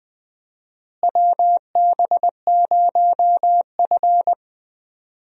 Morse code, heard as a keyed tone, about 20 wpm, 700 Hz. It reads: WB0F
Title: USA Call Signs - 4 Characters - Repeated 2x - 20wpm